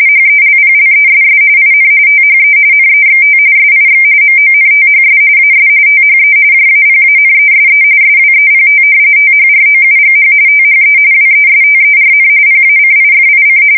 RTTY, about 45.45 baud. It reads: HELPLESS IN THE NEXT SIX MONTHS AS THE JAPANESE SWEPT THROUGH THE WESTERN PACIFIC
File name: fsk170 / usnavy4